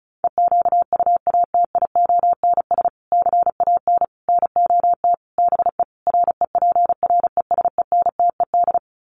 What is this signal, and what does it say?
Morse code, heard as a keyed tone, about 34 wpm, 700 Hz. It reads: EQUATIONS CAN NOT BE REPRESENTED
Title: Sets of 5 Words 35wpm